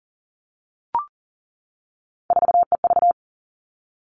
Morse code, heard as a keyed tone, about 40 wpm, 700 Hz. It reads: 4EV
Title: All Cut Numbers - 3-Digit 40wpm